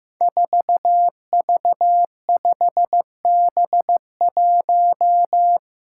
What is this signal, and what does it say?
Morse code, heard as a keyed tone, about 15 wpm, 700 Hz. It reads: 4V5B1
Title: Five Challenging Letter-Number Combinations 15wpm